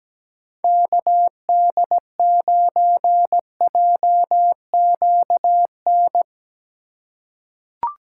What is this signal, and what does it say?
Morse code, heard as a keyed tone, about 17 wpm, 700 Hz. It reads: KD9JQN
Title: USA Call Signs - 6 Characters - 2x Character Spacing - 17wpm